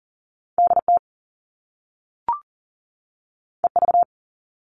Morse code, heard as a keyed tone, about 40 wpm, 700 Hz. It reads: DT EV
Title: All Cut Numbers - 2-Digit 40wpm